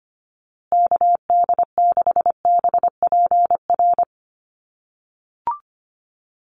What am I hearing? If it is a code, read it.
Morse code, heard as a keyed tone, about 25 wpm, 700 Hz. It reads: KD6BPR
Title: USA Call Signs Repeated 2x 25wpm